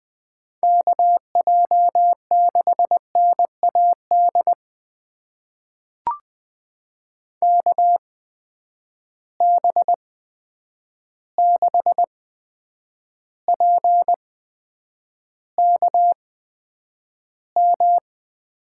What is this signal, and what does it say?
Morse code, heard as a keyed tone, about 20 wpm, spaced as for 9 wpm, 700 Hz. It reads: KJ6NAD KB6PKM
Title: USA Call Signs - 6 Characters - 8x Character Spacing - 20wpm